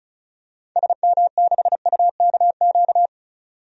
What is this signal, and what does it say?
Morse code, heard as a keyed tone, about 34 wpm, 700 Hz. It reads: SM6UKQ
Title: DXCC-100 Call Signs - 6 Characters - 35wpm